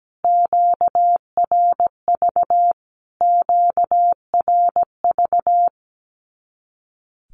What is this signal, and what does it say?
Morse code, heard as a keyed tone, about 17 wpm, 700 Hz. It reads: QRV QRV
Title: Q-Codes Repeated 3x 17wpm